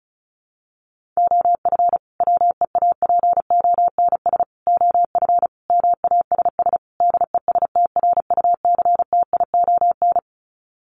Morse code, heard as a keyed tone, about 34 wpm, 700 Hz. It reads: OF WEAPONS OF MASS DESTRUCTION
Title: Sets of 5 Words Encore 35wpm